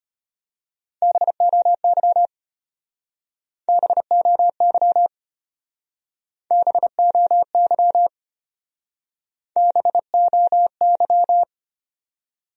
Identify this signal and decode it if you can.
Morse code, heard as a keyed tone, about 34 wpm, 700 Hz. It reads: BOY BOY BOY BOY